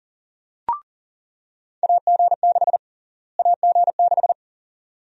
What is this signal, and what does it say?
Morse code, heard as a keyed tone, about 40 wpm, 700 Hz. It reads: AG6 AG6
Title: USA Call Sign Prefixes - Repeated 2x - 40wpm